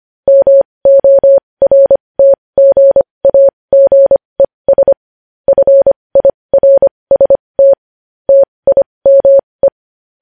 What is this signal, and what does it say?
Morse code, heard as a keyed tone, about 25 wpm, spaced as for 20 wpm, 550 Hz. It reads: MORTGAGES FIRST TIME
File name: podcast / News headlines-1x-25x20